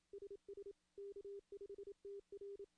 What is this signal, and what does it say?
Morse code, heard as a keyed tone, about 27 wpm, 400 Hz. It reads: SS K5TR
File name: contests / CW-SS-k5tr- -1500